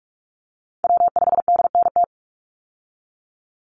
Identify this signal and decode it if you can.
Morse code, heard as a keyed tone, about 44 wpm, 700 Hz. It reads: W5DNT
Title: Call Signs Repeated 2x - 45wpm